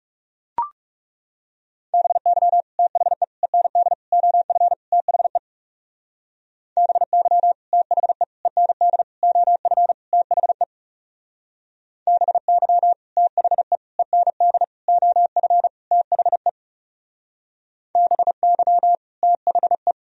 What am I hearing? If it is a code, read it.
Morse code, heard as a keyed tone, about 35 wpm, 700 Hz. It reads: BY THE END OF THE BY THE END OF THE BY THE END OF THE BY THE
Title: Speed-Racing - Sets of 5 Words 30wpm